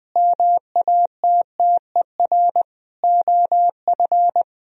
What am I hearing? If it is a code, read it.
Morse code, heard as a keyed tone, about 20 wpm, 700 Hz. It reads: MATTER OF
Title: Sentences from Top 1000 Words - Review Part 3 - 20wpm